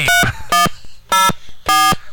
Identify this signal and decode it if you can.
DTMF, heard as telephone touch tones, tones of about 170 ms. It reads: 3104